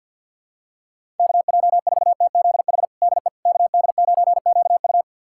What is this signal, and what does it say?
Morse code, heard as a keyed tone, about 52 wpm, 690 Hz. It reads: KJ4TZH DE KD9QU
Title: USA Callsign DE Callsign 50wpm